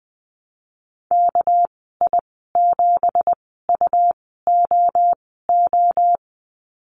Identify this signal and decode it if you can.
Morse code, heard as a keyed tone, about 20 wpm, spaced as for 15 wpm, 700 Hz. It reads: KI7UOO
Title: 200 Lesson 32g - Letter k - Call Signs 20wpm